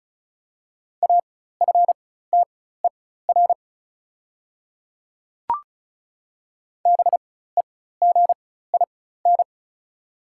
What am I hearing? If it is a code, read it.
Morse code, heard as a keyed tone, about 35 wpm, spaced as for 16 wpm, 700 Hz. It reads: AFTER BEGIN
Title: 4x Character Spacing - Top 500 Words - 35wpm